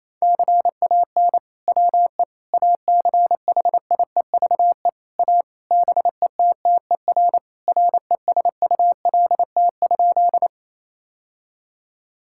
Morse code, heard as a keyed tone, about 28 wpm, 700 Hz. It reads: CAN WE ACHIEVE A BETTER RESULT?
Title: Sentences from Top 1000 Words - Review Part 6 - 28wpm